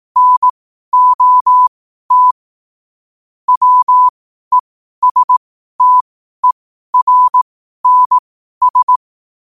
Morse code, heard as a keyed tone, about 18 wpm, spaced as for 12 wpm, 1000 Hz. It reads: NOT WESTERNS
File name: qotdfiles / 12wpm